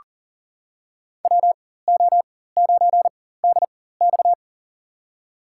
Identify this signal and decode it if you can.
Morse code, heard as a keyed tone, about 40 wpm, spaced as for 19 wpm, 700 Hz. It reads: WO9DX
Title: USA Call Signs - 5 Characters - 4x Character Spacing - 40wpm